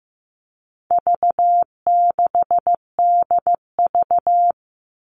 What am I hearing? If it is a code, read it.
Morse code, heard as a keyed tone, about 15 wpm, 700 Hz. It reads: V6DV
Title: All Cut Numbers - 4-Digit 15wpm